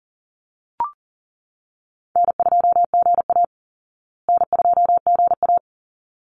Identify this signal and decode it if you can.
Morse code, heard as a keyed tone, about 38 wpm, 700 Hz. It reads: N2GA N2GA
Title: Easy Call Signs Repeated 2x - 40wpm